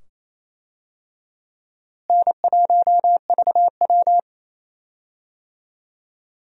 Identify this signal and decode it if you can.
Morse code, heard as a keyed tone, about 28 wpm, 700 Hz. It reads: N1VW
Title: USA Call Signs - 4 Characters - Repeated 2x - 28wpm